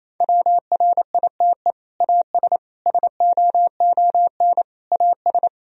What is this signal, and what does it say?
Morse code, heard as a keyed tone, about 28 wpm, 700 Hz. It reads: WRITE AS SOON AS
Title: Sentences from Top 1000 Words - Review Part 1 - 28wpm